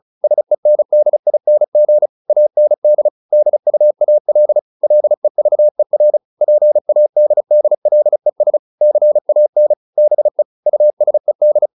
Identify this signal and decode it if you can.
Morse code, heard as a keyed tone, about 34 wpm, 600 Hz. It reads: SENDING AND DUAL LEVER PADDLES CAN BE USED